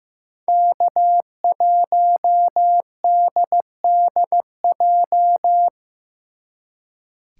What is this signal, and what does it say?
Morse code, heard as a keyed tone, about 18 wpm, 700 Hz. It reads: K1DDJ
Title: Speed Racing - USA 1x3 Call Signs 15wpm